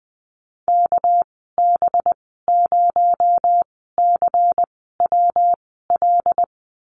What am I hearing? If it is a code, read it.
Morse code, heard as a keyed tone, about 20 wpm, spaced as for 14 wpm, 700 Hz. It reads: KB0CWL
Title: 238 Lesson 38g - Letter q - Call Signs 20wpm